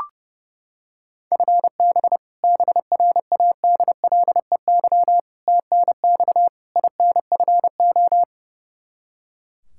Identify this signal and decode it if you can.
Morse code, heard as a keyed tone, about 32 wpm, 700 Hz. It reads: FB BRADLEY TNX INFO
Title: Practice QSO - Exchange 3 - More Info - 30wpm